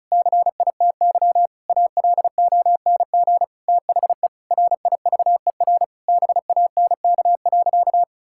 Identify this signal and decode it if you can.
Morse code, heard as a keyed tone, about 36 wpm, 700 Hz. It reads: CITY ALONG THE RIVER BANK.
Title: Sentences from Top 1000 Words - Review Part 6 - 35wpm